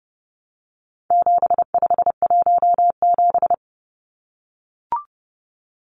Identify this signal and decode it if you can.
Morse code, heard as a keyed tone, about 30 wpm, 700 Hz. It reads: 7517
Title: Numbers - 4-Digit 30wpm